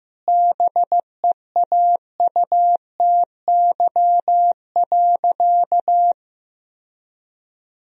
Morse code, heard as a keyed tone, about 15 wpm, 700 Hz. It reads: BEAUTY.
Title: Sentences from Top 1000 Words - Review Part 3 - 15wpm